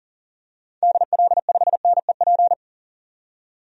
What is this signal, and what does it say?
Morse code, heard as a keyed tone, about 38 wpm, 700 Hz. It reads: DL5NEP
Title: DXCC-100 Call Signs - 6 Characters - 40wpm